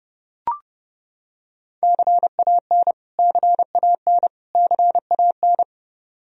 Morse code, heard as a keyed tone, about 32 wpm, 700 Hz. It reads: CAN CAN CAN
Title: Top 100 Words Repeated 30wpm